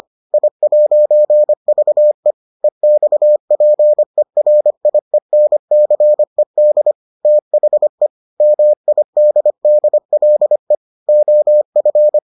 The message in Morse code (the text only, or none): I'VE EXPERIENCED THE MIDDLE OF